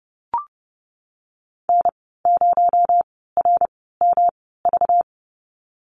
Morse code, heard as a keyed tone, about 30 wpm, spaced as for 17 wpm, 700 Hz. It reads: N0RMV
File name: Lesson 37g - Letter x - Call Signs 30wpm